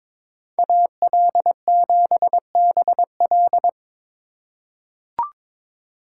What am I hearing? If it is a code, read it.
Morse code, heard as a keyed tone, about 22 wpm, 700 Hz. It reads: AL7BL
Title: USA Call Signs - 5 Characters - 4x Character Spacing - 22wpm